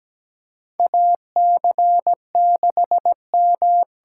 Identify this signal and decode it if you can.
Morse code, heard as a keyed tone, about 17 wpm, 700 Hz. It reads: AC6M
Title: USA Call Signs - 5 Characters - 8x Character Spacing - 17wpm